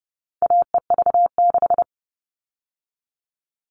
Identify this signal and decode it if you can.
Morse code, heard as a keyed tone, about 30 wpm, 700 Hz. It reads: AEV6
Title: All Cut Numbers - 4-Digit 30wpm